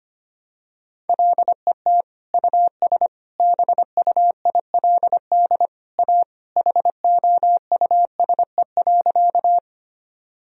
Morse code, heard as a keyed tone, about 25 wpm, 700 Hz. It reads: LET US BUILD A HOUSE.
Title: Sentences from Top 1000 Words - Review Part 1 - 25wpm